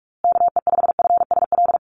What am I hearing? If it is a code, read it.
Morse code, heard as a keyed tone, about 44 wpm, 700 Hz. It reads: KE5FSL
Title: USA Call Signs Repeated 3x 45wpm